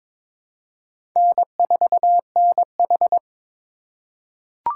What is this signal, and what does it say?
Morse code, heard as a keyed tone, about 22 wpm, 700 Hz. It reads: N4NH
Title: USA Call Signs - 4 Characters - 4x Character Spacing - 22wpm